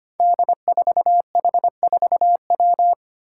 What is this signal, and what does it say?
Morse code, heard as a keyed tone, about 25 wpm, 700 Hz. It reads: D4H4W
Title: Five Challenging Letter-Number Combinations 25wpm